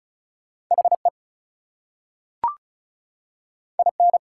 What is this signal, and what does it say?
Morse code, heard as a keyed tone, about 36 wpm, 700 Hz. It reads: HE IN
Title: Two Letter Words - 35wpm